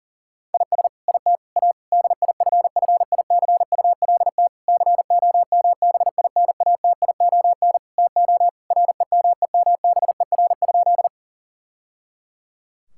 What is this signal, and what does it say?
Morse code, heard as a keyed tone, about 40 wpm, 700 Hz. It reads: IS IT A DIFFICULT COMBINATION TO REMEMBER?
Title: Sentences from Top 1000 Words - Review Part 6 - 40wpm